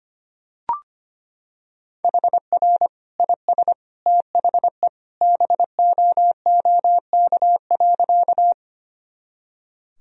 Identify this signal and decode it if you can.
Morse code, heard as a keyed tone, about 25 wpm, 700 Hz. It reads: HR IS THE BOOK.
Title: Sentences from Top 300 Words and Abbreviations 25wpm